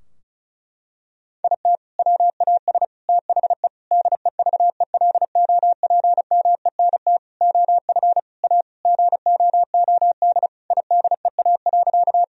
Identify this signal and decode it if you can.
Morse code, heard as a keyed tone, about 36 wpm, 700 Hz. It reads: IT WAS THE DEVELOPMENT OF A GOOD IDEA.
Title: Sentences from Top 1000 Words - Review Part 4 - 35wpm